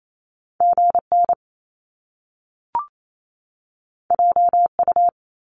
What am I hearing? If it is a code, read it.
Morse code, heard as a keyed tone, about 28 wpm, 700 Hz. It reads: GN JU
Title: Top Two Letter Combinations in Words 28wpm